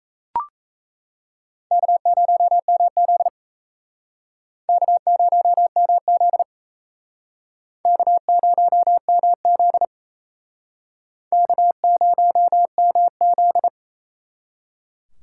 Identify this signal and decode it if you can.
Morse code, heard as a keyed tone, about 38 wpm, 700 Hz. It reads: K0MZ K0MZ K0MZ K0MZ